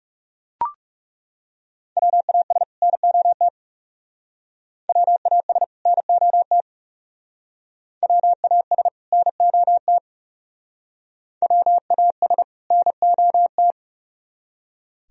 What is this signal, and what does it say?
Morse code, heard as a keyed tone, about 35 wpm, 700 Hz. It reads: WAS NOT WAS NOT WAS NOT WAS NOT
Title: Speed-Racing - Sets of 2 Words 30wpm